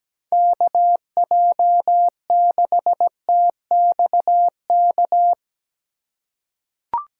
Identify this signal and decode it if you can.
Morse code, heard as a keyed tone, about 17 wpm, 700 Hz. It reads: KJ6TXK
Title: USA Call Signs - 6 Characters - 4x Character Spacing - 17wpm